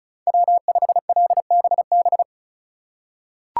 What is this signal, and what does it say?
Morse code, heard as a keyed tone, about 36 wpm, 700 Hz. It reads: W5LBB
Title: USA Call Signs - 5 Characters - 3x Character Spacing - 35wpm